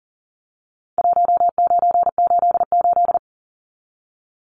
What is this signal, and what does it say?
Morse code, heard as a keyed tone, about 40 wpm, 700 Hz. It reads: 1988
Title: Numbers - Contest Style - 4-Digit 40wpm